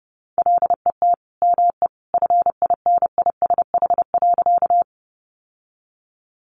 Morse code, heard as a keyed tone, about 28 wpm, 700 Hz. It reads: LET ME FINISH.
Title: Sentences from Top 1200 Words 30wpm